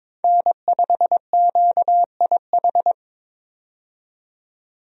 Morse code, heard as a keyed tone, about 22 wpm, 700 Hz. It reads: N5QIH
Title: USA Call Signs - 5 Characters - 22wpm